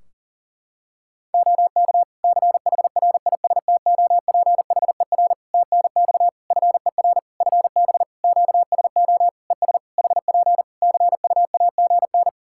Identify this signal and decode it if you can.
Morse code, heard as a keyed tone, about 40 wpm, 700 Hz. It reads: OK CHRISTOPHER TNX FER FB QSO ES HP CUAGN
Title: Practice QSO - Exchange 4 - Ending - 40wpm